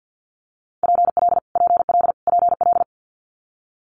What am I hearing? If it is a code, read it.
Morse code, heard as a keyed tone, about 50 wpm, 700 Hz. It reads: PL PL PL
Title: Multi-letter Phonemes Repeated 3x 50wpm